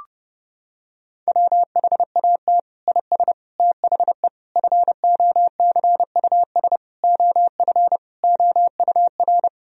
Morse code, heard as a keyed tone, about 32 wpm, 700 Hz. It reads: WHAT IS THE FOCUS OF OUR
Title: Sentences from Top 1000 Words - Review Part 4 - 30wpm